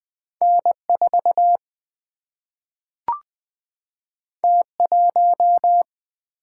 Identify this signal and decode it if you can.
Morse code, heard as a keyed tone, about 20 wpm, 700 Hz. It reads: N 4 T 1
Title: Single-Letter Single-Number 20wpm